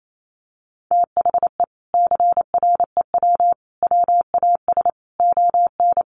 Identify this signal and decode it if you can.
Morse code, heard as a keyed tone, about 28 wpm, 700 Hz. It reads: THE CREW WAS ON
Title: Sentences from Top 1500 Words 28wpm